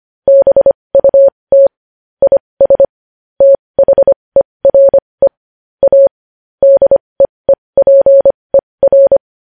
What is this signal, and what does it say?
Morse code, heard as a keyed tone, about 25 wpm, spaced as for 20 wpm, 550 Hz. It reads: BUT IS THERE A DEEPER
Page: News Headlines in Morse Code at 20 WPM